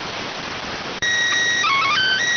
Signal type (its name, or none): SSTV (slow-scan television)